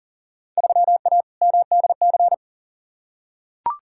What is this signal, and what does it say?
Morse code, heard as a keyed tone, about 40 wpm, 700 Hz. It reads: 3A MDC
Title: Field Day Exchanges - 40wpm